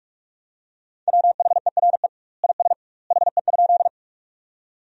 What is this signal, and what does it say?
Morse code, heard as a keyed tone, about 42 wpm, 700 Hz. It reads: WHERE IS HE?
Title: 2x Word-Spacing - Sentences from Top 200 Words - 45wpm